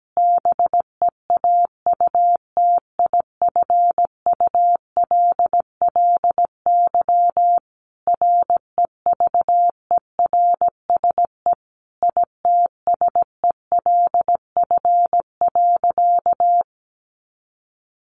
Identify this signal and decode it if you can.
Morse code, heard as a keyed tone, about 17 wpm, 700 Hz. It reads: BEAUTIFULLY REVERSE ITSELF.
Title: A Sound Of Thunder 17wpm